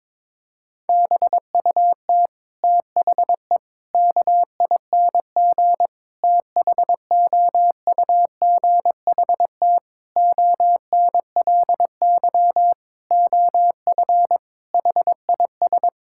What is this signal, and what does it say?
Morse code, heard as a keyed tone, about 22 wpm, 700 Hz. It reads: BUT THE KING THOUGHT ONLY OF HIS
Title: Sentences from Top 1600 Words 22wpm